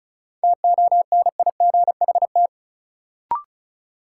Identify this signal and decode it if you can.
Morse code, heard as a keyed tone, about 36 wpm, 700 Hz. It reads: TONIGHT
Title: Sentences from Top 600 Words - New Word Drill - 35wpm